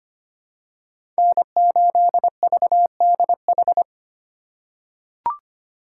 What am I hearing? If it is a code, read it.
Morse code, heard as a keyed tone, about 25 wpm, 700 Hz. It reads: N8VDH
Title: USA Call Signs - 5 Characters - 8x Character Spacing - 25wpm